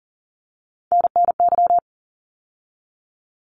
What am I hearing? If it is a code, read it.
Morse code, heard as a keyed tone, about 40 wpm, 700 Hz. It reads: NNY
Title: ARRL Sections 40wpm